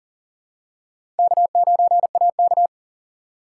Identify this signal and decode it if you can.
Morse code, heard as a keyed tone, about 40 wpm, 700 Hz. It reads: K9AK
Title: USA Call Signs - 4x Character Spacing - 40wpm